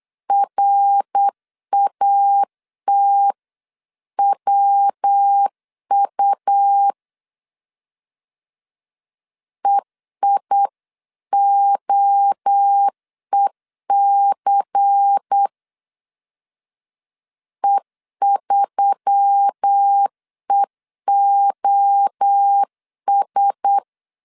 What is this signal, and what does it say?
Morse code, heard as a keyed tone, about 8 wpm, 800 Hz. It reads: RATWU EIOEC E3EOS
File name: CW / DET / 1 6 1500C S B 8wpm 5142